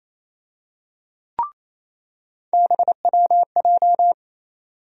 Morse code, heard as a keyed tone, about 28 wpm, 700 Hz. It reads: BWJ
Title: USA Call Sign Postfixes - 3 Letters 28wpm